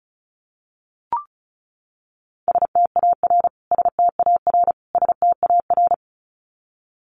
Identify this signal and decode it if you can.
Morse code, heard as a keyed tone, about 34 wpm, 700 Hz. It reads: STAR STAR STAR
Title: Top 500 Words Repeated 35wpm